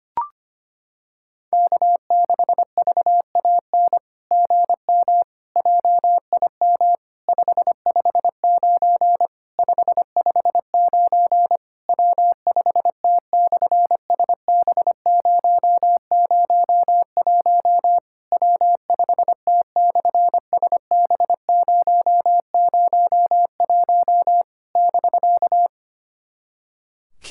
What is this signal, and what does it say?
Morse code, heard as a keyed tone, about 25 wpm, 700 Hz. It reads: K6VAN GM JIM 559 559 W5T/SB001 W5T/SB001 <BK>
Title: SOTA QSO Practice 25wpm